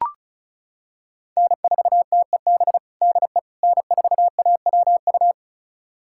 Morse code, heard as a keyed tone, about 36 wpm, 700 Hz. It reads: N4TEB DE N4AWU
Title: USA Callsign DE Callsign - 5 Characters - 35wpm